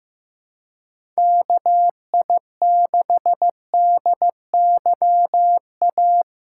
Morse code, heard as a keyed tone, about 15 wpm, 700 Hz. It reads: KI6DYA
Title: USA Call Signs - 6 Characters - 4x Character Spacing - 15wpm